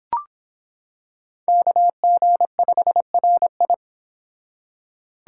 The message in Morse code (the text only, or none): KG5RI